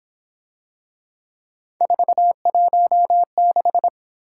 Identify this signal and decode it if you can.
Morse code, heard as a keyed tone, about 26 wpm, 700 Hz. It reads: 416
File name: Speed-Racing - Three Letter-Number Combinations - 22wpm